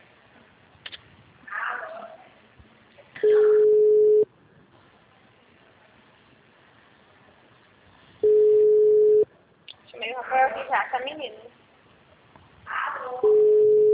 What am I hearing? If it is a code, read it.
Ringback tone (Europe, 425 Hz), with speech.
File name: out- -2025- - - .40074